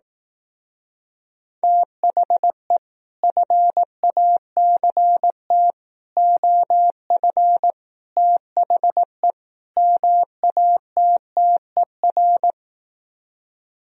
Morse code, heard as a keyed tone, about 18 wpm, 700 Hz. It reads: THE FACT OF THE MATTER
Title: Speed-Racing - Sets of 5 Words 15wpm